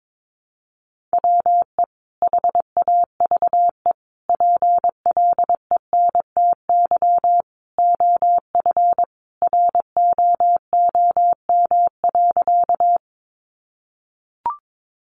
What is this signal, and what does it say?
Morse code, heard as a keyed tone, about 22 wpm, 700 Hz. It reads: WE HAVE PLENTY OF ROOM.
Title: Sentences from Top 1300 Words 22wpm